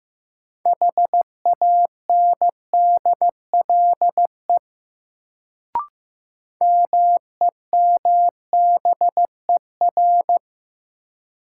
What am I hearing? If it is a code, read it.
Morse code, heard as a keyed tone, about 15 wpm, 700 Hz. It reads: HANDLE MEMBER
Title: Sentences from Top 1100 Words - New Word Drill Alt - 15wpm